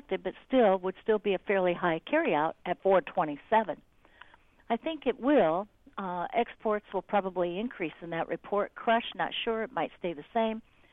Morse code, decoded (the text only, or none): none